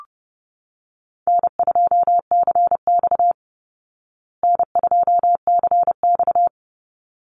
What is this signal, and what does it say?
Morse code, heard as a keyed tone, about 30 wpm, 700 Hz. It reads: N2CX N2CX
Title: Easy Call Signs Repeated 2x - 30wpm